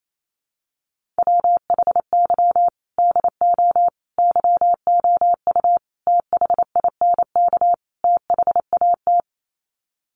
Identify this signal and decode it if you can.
Morse code, heard as a keyed tone, about 28 wpm, 700 Hz. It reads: WHY DO YOU THINK THAT
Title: Sets of 5 Words Encore2 28wpm